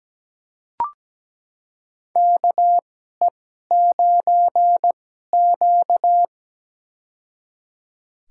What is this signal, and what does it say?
Morse code, heard as a keyed tone, about 17 wpm, spaced as for 12 wpm, 700 Hz. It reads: KE9Q
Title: USA Call Signs - 4 Characters - 2x Character Spacing - 17wpm